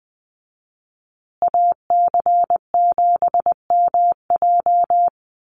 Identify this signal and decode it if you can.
Morse code, heard as a keyed tone, about 20 wpm, 700 Hz. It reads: AC7MJ
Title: USA Call Signs Repeated 2x 20wpm